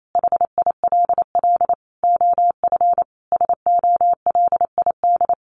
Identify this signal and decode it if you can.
Morse code, heard as a keyed tone, about 28 wpm, 700 Hz. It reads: HILL OF SOLID
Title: A Sound Of Thunder 28wpm